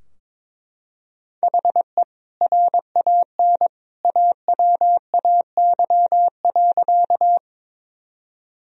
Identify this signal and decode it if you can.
Morse code, heard as a keyed tone, about 22 wpm, 700 Hz. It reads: HE RAN AWAY.
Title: Sentences from Top 1000 Words - Review Part 3 - 22wpm